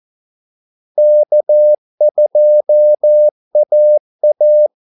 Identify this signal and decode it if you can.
Morse code, heard as a keyed tone, about 14 wpm, 600 Hz. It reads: K2AA
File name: Callsigns-14wpm